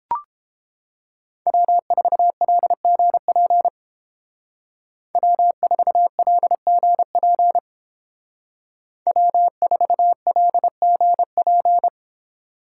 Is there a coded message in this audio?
Yes: Morse code from 1.5 to 11.9 s, heard as a keyed tone, about 29 wpm, 700 Hz. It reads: W4LGP W4LGP W4LGP